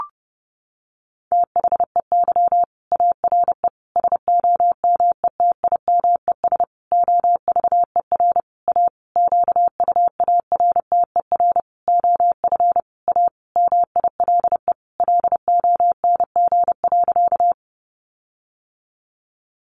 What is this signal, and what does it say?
Morse code, heard as a keyed tone, about 30 wpm, 700 Hz. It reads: THEY ARE SOMETIMES OVER A QUARTER OF A MILE LONG.
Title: Sentences from Top 1300 Words 30wpm